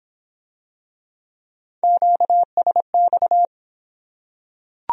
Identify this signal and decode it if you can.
Morse code, heard as a keyed tone, about 26 wpm, 700 Hz. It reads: QSX